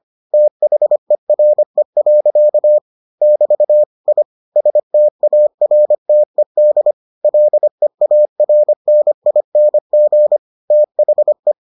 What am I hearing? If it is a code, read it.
Morse code, heard as a keyed tone, about 25 wpm, 600 Hz. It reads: THERE. <BT> I STARTED LEARNING THE